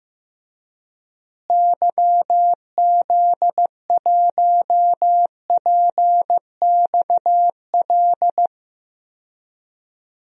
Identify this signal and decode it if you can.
Morse code, heard as a keyed tone, about 15 wpm, 700 Hz. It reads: YZ1PXL
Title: DXCC-100 Call Signs - 6 Characters - Repeated 3x - 15wpm